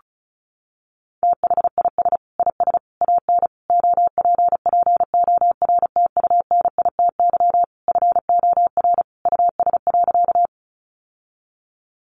Morse code, heard as a keyed tone, about 34 wpm, 700 Hz. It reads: THIS IS AN OPPORTUNITY FOR US.
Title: Sentences from Top 1000 Words 35wpm